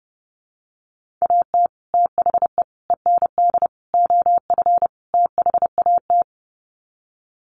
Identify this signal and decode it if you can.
Morse code, heard as a keyed tone, about 30 wpm, 700 Hz. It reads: AT THE END OF THAT
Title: Sets of 5 Words Encore2 30wpm